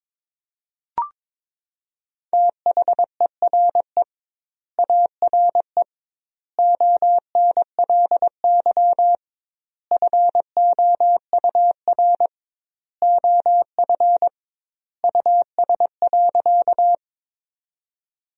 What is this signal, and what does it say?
Morse code, heard as a keyed tone, about 22 wpm, 700 Hz. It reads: THERE ARE ONLY FOUR OF US.
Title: 2x Word-Spacing - Sentences from Top 300 Words - 22wpm